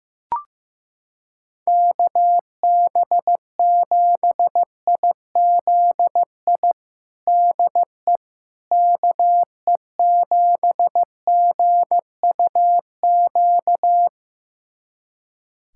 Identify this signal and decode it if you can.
Morse code, heard as a keyed tone, about 15 wpm, 700 Hz. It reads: KB7IZI DE KE7GUQ
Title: USA Callsign DE Callsign - 6 Characters - 15wpm